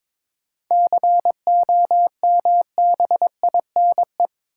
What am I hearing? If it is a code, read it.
Morse code, heard as a keyed tone, about 23 wpm, 700 Hz. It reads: COMBINE
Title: Sentences from Top 1600 Words - New Word Drill - 22wpm